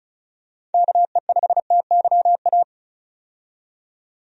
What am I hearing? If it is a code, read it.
Morse code, heard as a keyed tone, about 36 wpm, 700 Hz. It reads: KE5TYA
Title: USA Call Signs - 6 Characters - 35wpm